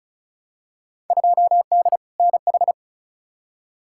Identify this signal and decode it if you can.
Morse code, heard as a keyed tone, about 36 wpm, 690 Hz. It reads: 2D NH
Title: Field Day Exchanges Encore - 35wpm